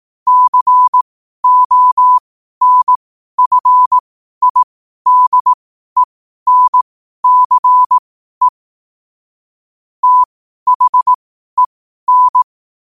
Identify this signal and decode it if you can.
Morse code, heard as a keyed tone, about 18 wpm, spaced as for 12 wpm, 1000 Hz. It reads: CONFIDENCE THEN